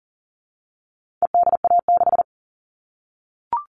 Morse code, heard as a keyed tone, about 40 wpm, 700 Hz. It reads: EDA6
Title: All Cut Numbers - 4-Digit 40wpm